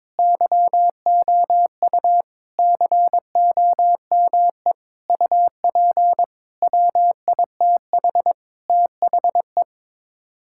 Morse code, heard as a keyed tone, about 22 wpm, 700 Hz. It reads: YOU COME UP WITH THE
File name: practice / 22wpm / Sets of 7 Words - 22wpm